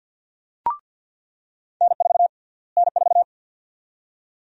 Morse code, heard as a keyed tone, about 47 wpm, 700 Hz. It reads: N4 N4
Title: USA Call Sign Prefixes - Repeated 2x - 50wpm